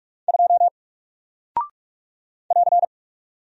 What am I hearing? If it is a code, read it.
Morse code, heard as a keyed tone, about 44 wpm, 700 Hz. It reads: 2 <AR>
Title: Single Character (Letter:Number:Punctuation:Prosign) Alt 45wpm